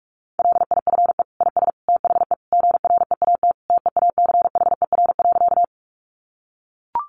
Morse code, heard as a keyed tone, about 44 wpm, 700 Hz. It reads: LIFE IS THE GREAT TEACHER.
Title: Sentences from Top 900 Words 45wpm